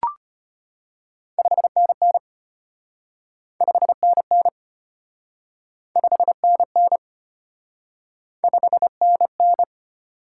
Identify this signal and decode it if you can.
Morse code, heard as a keyed tone, about 33 wpm, 700 Hz. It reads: 5NN 5NN 5NN 5NN